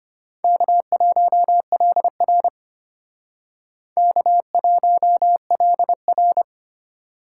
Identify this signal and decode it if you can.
Morse code, heard as a keyed tone, about 32 wpm, 700 Hz. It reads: K1LR K1LR